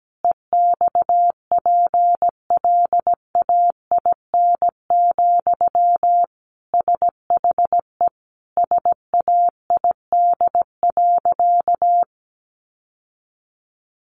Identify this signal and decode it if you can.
Morse code, heard as a keyed tone, about 17 wpm, 700 Hz. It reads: EXPLAIN, SHE SAID.
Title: Sentences from Top 1100 Words 17wpm